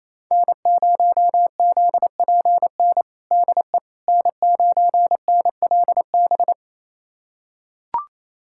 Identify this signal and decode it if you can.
Morse code, heard as a keyed tone, about 28 wpm, 700 Hz. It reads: N0ZPN DE N9NLB
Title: USA Callsign DE Callsign 28wpm